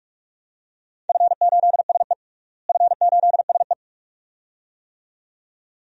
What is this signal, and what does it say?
Morse code, heard as a keyed tone, about 43 wpm, 700 Hz. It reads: F8SE F8SE
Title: DXCC-100 Call Signs - 4 Characters - Repeated 2x - 45wpm